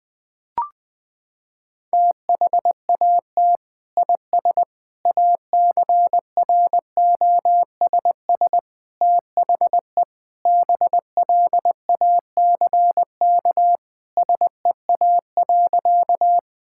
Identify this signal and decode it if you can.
Morse code, heard as a keyed tone, about 21 wpm, 700 Hz. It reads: THAT IS ACROSS THE BLACK SEA.
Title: Sentences from Top 1000 Words - Review Part 5 - 20wpm